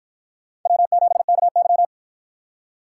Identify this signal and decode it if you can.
Morse code, heard as a keyed tone, about 50 wpm, 700 Hz. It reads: W7KY